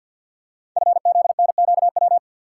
Morse code, heard as a keyed tone, about 48 wpm, 700 Hz. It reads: FZNQW
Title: Code Groups - 5 Letters - 50wpm